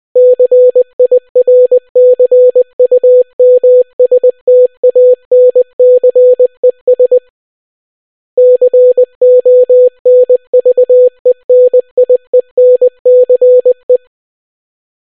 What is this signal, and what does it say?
Morse code, heard as a keyed tone, about 20 wpm, 500 Hz. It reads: CIRCUMSTANCES CONVENIENCE